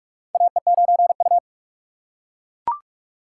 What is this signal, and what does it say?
Morse code, heard as a keyed tone, about 44 wpm, 690 Hz. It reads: AE9U
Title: USA Call Signs - 4 Characters - 3x Character Spacing - 45wpm